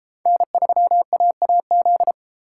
Morse code, heard as a keyed tone, about 31 wpm, 700 Hz. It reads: N3AAZ